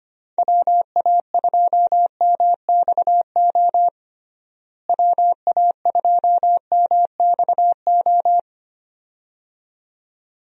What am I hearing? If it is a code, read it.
Morse code, heard as a keyed tone, about 25 wpm, 700 Hz. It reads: WA2MXO WA2MXO
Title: USA Call Signs - 6 Characters - Repeated 2x - 25wpm